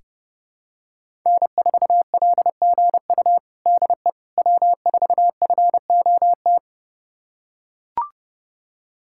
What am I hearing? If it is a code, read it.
Morse code, heard as a keyed tone, about 32 wpm, 700 Hz. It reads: N4LGU DE W4FOT
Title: USA Callsign DE Callsign - 5 Characters - 30wpm